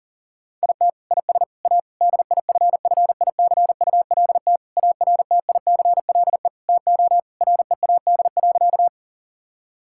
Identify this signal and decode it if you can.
Morse code, heard as a keyed tone, about 40 wpm, 700 Hz. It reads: IT IS A DIFFICULT ARTICLE TO READ.
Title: Sentences from Top 1000 Words - Review Part 5 - 40wpm